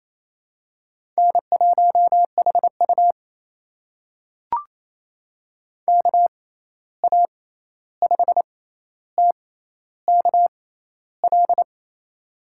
Morse code, heard as a keyed tone, about 28 wpm, spaced as for 13 wpm, 700 Hz. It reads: N1HU KA5TKL
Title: USA Call Signs - 6x Character Spacing - 28wpm